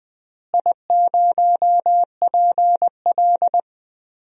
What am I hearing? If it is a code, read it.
Morse code, heard as a keyed tone, about 20 wpm, 700 Hz. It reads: I0PL
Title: DXCC-100 Call Signs - 4 Characters - Repeated 3x - 20wpm